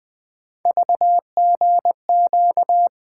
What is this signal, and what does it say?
Morse code, heard as a keyed tone, about 20 wpm, 700 Hz. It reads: VGQ
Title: Three Letter-Number Combinations - 20wpm